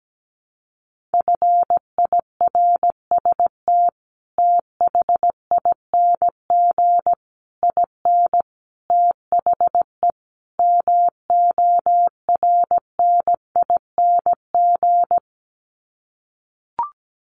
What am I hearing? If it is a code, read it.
Morse code, heard as a keyed tone, about 17 wpm, 700 Hz. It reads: FIRST THING IN THE MORNING
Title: Sets of 5 Words Encore2 17wpm